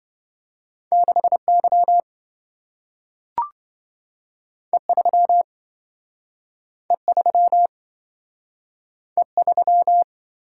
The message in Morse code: 6Y E3 E3 E3

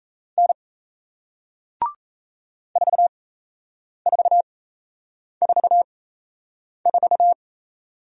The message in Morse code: N 4444

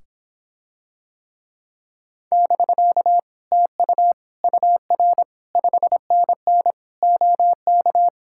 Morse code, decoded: <BK> TU UR 5NN OK

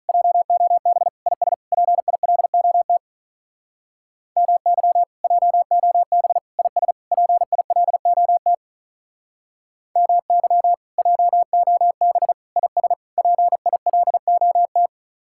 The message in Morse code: JOB IS PILOT MY JOB IS PILOT MY JOB IS PILOT

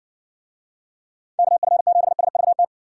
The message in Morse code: KF7SVT

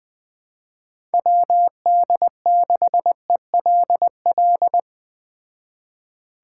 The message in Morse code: WD6ELL

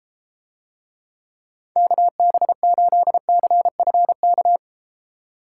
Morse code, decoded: KB8CFK

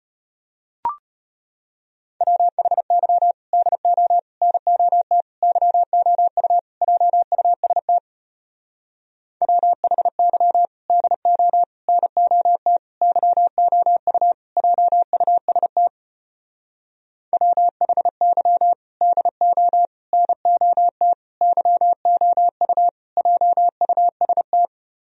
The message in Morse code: WHY DO NOT YOU JUST WHY DO NOT YOU JUST WHY DO NOT YOU JUST

